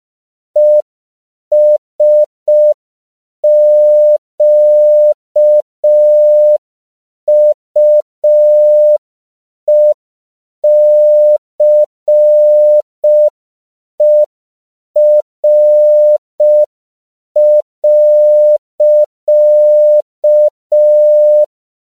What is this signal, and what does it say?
Morse code, heard as a keyed tone, about 5 wpm, 600 Hz. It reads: ESQUECER.
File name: CW / 5PPM